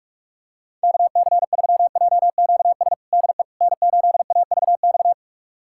Morse code, heard as a keyed tone, about 44 wpm, 700 Hz. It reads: KC3JQS DE N8AVX
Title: USA Callsign DE Callsign 45wpm